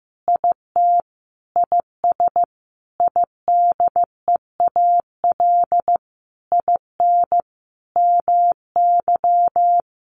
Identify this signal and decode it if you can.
Morse code, heard as a keyed tone, about 15 wpm, 700 Hz. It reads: IT IS IDEAL IN MY
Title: Sentences from Top 1400 Words 15wpm